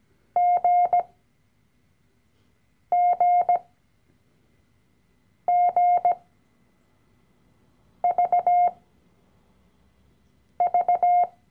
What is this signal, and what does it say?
Morse code, heard as a keyed tone, about 17 wpm, spaced as for 4 wpm, 710 Hz. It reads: GGGVV
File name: CW-Online-Kurs / Anfänger / Lektion 012